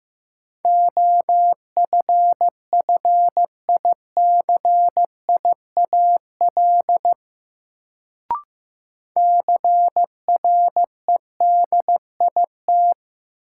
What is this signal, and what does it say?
Morse code, heard as a keyed tone, about 15 wpm, 700 Hz. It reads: OFFICIAL CREDIT
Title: Sentences from Top 1000 Words - New Word Drill - 15wpm